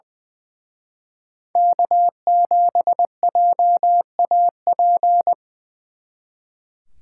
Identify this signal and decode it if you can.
Morse code, heard as a keyed tone, about 24 wpm, 700 Hz. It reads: K7JAP
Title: Speed Racing - USA 1x3 Call Signs 20wpm